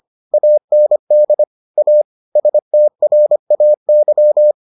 Morse code, heard as a keyed tone, about 25 wpm, 600 Hz. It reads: AND A STRAY